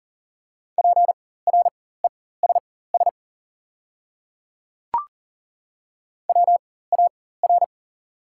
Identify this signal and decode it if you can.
Morse code, heard as a keyed tone, about 38 wpm, spaced as for 18 wpm, 700 Hz. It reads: PRESS WAR
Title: 4x Character Spacing - Top 500 Words - 40wpm